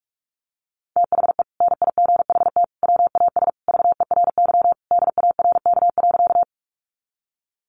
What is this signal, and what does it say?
Morse code, heard as a keyed tone, about 44 wpm, 700 Hz. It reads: THE NIGHT WAS VERY DARK.
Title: Sentences from Top 500 Words 45wpm